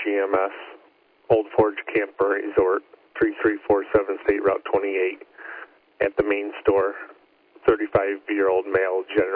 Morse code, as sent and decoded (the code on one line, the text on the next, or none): none